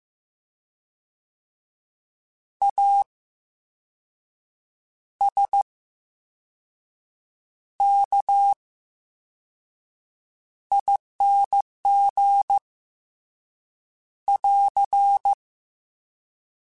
Morse code, heard as a keyed tone, about 15 wpm, spaced as for 5 wpm, 800 Hz. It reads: ASKING <AR>
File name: Special INGs / Special INGs-mp3s / Slow ing @13